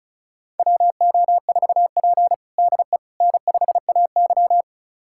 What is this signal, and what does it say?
Morse code, heard as a keyed tone, about 36 wpm, 700 Hz. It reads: WO4P DE N5AY
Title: USA Callsign DE Callsign - 4 Characters - 35wpm